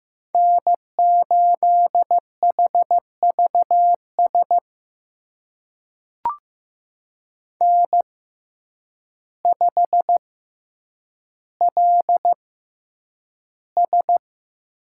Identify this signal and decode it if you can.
Morse code, heard as a keyed tone, about 15 wpm, spaced as for 8 wpm, 700 Hz. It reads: N8HVS N5LS
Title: USA Call Signs - 6x Character Spacing - 15wpm